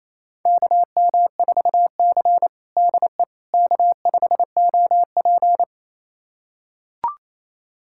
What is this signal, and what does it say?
Morse code, heard as a keyed tone, about 27 wpm, 700 Hz. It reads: KM4C DE K5OP
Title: USA Callsign DE Callsign - 4 Characters - 28wpm